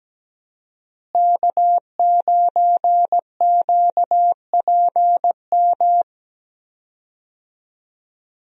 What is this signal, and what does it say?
Morse code, heard as a keyed tone, about 17 wpm, 700 Hz. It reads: K9QPM
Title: USA Call Signs - 5 Characters - Repeated 2x - 17wpm